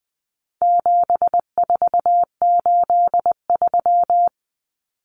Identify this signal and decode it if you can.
Morse code, heard as a keyed tone, about 20 wpm, 700 Hz. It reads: 7483
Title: Common Cut Numbers - 4-Digit 20wpm